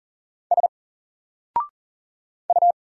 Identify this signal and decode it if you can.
Morse code, heard as a keyed tone, about 39 wpm, 700 Hz. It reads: S U
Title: Single Letter-Number Alt 40wpm